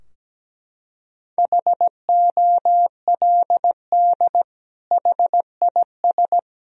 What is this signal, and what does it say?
Morse code, heard as a keyed tone, about 17 wpm, 700 Hz. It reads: HOLD HIS
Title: Sentences from Top 1000 Words - Review Part 3 - 17wpm